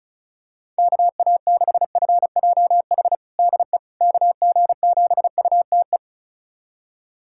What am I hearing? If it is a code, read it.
Morse code, heard as a keyed tone, about 36 wpm, 700 Hz. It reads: KA6FJH DE KG7UTE
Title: USA Callsign DE Callsign - 6 Characters - 35wpm